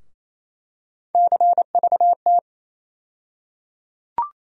Morse code, heard as a keyed tone, about 28 wpm, 700 Hz. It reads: CVT
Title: USA Call Sign Postfixes - 3 Letters 28wpm